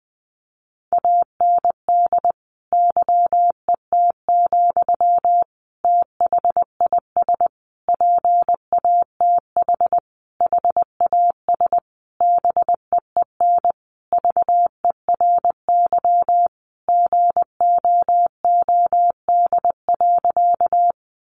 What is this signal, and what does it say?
Morse code, heard as a keyed tone, about 20 wpm, 700 Hz. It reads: AND YET, THIS PATH HAS BEEN VERY GOOD.
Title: Sentences from Top 1400 Words 20wpm